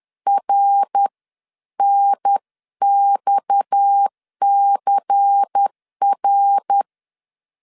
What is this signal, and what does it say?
Morse code, heard as a keyed tone, about 10 wpm, spaced as for 9 wpm, 800 Hz. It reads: RNXCR
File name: CW / DET / 1 6 1500C S B 10wpm 4137